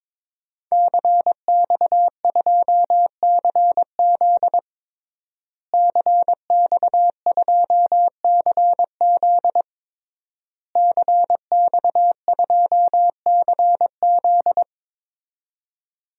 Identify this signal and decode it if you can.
Morse code, heard as a keyed tone, about 21 wpm, 700 Hz. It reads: CX2CZ CX2CZ CX2CZ
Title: DXCC-100 Call Signs - 5 Characters - Repeated 3x - 22wpm